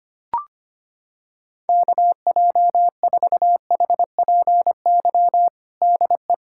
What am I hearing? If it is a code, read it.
Morse code, heard as a keyed tone, about 25 wpm, 700 Hz. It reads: KJ4HPY DE
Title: USA Callsign DE Callsign 25wpm